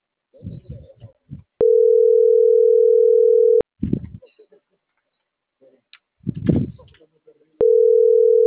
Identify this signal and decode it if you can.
Ringback tone (North America), 440 + 480 Hz.